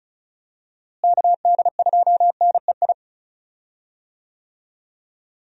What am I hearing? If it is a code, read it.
Morse code, heard as a keyed tone, about 35 wpm, 700 Hz. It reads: KD2NEI
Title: USA Call Signs - 6 Characters - Repeated 2x - 35wpm